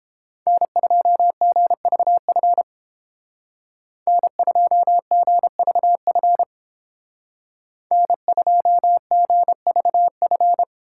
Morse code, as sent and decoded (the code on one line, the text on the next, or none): -. ..--- --. ...- ..-. / -. ..--- --. ...- ..-. / -. ..--- --. ...- ..-.
N2GVF N2GVF N2GVF